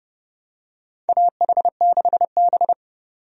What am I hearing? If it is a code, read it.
Morse code, heard as a keyed tone, about 32 wpm, 700 Hz. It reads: AH6B
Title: USA Call Signs - 4 Characters - 30wpm